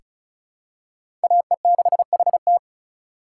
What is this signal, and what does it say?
Morse code, heard as a keyed tone, about 36 wpm, 700 Hz. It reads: AE6HT
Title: USA Call Signs - 2x Character Spacing - 35wpm